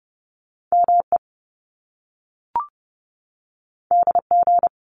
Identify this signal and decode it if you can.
Morse code, heard as a keyed tone, about 30 wpm, 700 Hz. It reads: ME DG
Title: Top Two Letter Combinations in Words 30wpm